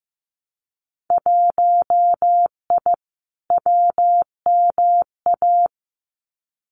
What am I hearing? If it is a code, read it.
Morse code, heard as a keyed tone, about 15 wpm, 700 Hz. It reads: 1I WMA
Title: Winter Field Day Exchanges Encore 15wpm